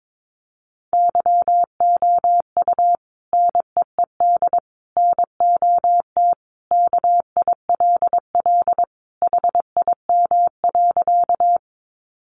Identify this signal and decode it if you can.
Morse code, heard as a keyed tone, about 22 wpm, 700 Hz. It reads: YOU NEED NOT KILL HIM.
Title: Sentences from Top 1400 Words 22wpm